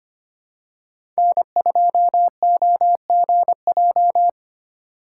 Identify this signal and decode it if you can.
Morse code, heard as a keyed tone, about 25 wpm, 700 Hz. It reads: N2OGJ
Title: USA Call Signs - 3x Character Spacing - 25wpm